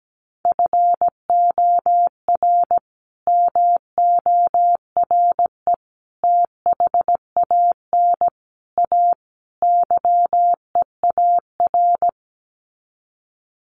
Morse code, heard as a keyed tone, about 17 wpm, 700 Hz. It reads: FOR MORE THAN A YEAR
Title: Sets of 5 Words 17wpm